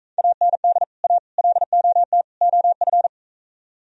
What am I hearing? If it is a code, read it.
Morse code, heard as a keyed tone, about 43 wpm, 700 Hz. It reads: AND A LOT OF